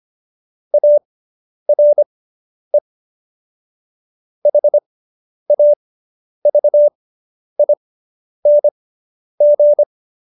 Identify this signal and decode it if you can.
Morse code, heard as a keyed tone, about 25 wpm, spaced as for 10 wpm, 590 Hz. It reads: ARE HAVING